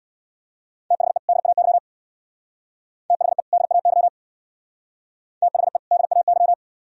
Morse code, heard as a keyed tone, about 63 wpm, 700 Hz. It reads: THE DAY THE DAY THE DAY